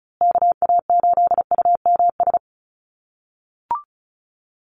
Morse code, heard as a keyed tone, about 34 wpm, 700 Hz. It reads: KA8UMS
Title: USA Call Signs Repeated 2x 35wpm